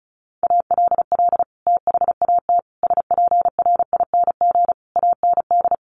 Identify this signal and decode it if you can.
Morse code, heard as a keyed tone, about 34 wpm, 700 Hz. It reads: ALL THAT SPRING AND
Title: Sentences from Top 1200 Words 35wpm